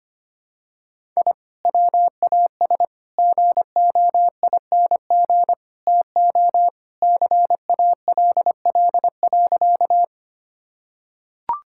Morse code, heard as a keyed tone, about 25 wpm, 700 Hz. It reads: I WAS GOING TO CALL.
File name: practice / 25wpm / Sentences with 4-Word Phrases - 25wpm